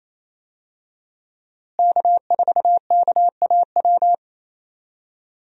Morse code, heard as a keyed tone, about 28 wpm, 700 Hz. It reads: K4KAW
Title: USA Call Signs - 5 Characters - Repeated 2x - 28wpm